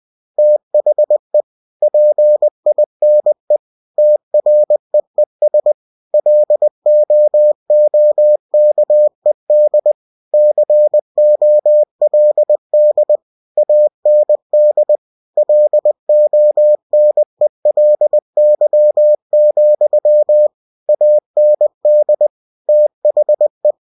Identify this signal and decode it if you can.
Morse code, heard as a keyed tone, about 20 wpm, 600 Hz. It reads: THE PINE TREES LOOKED COLD AND LONELY, AND THE
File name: cw-books / winnie-the-pooh / winnie-20wpm-page-0040